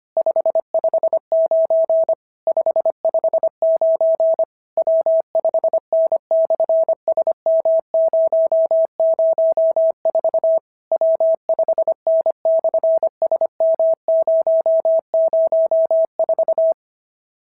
Morse code, heard as a keyed tone, about 25 wpm, 690 Hz. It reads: 559 559 W5N/SM004 W5N/SM004
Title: SOTA QSO Practice 25wpm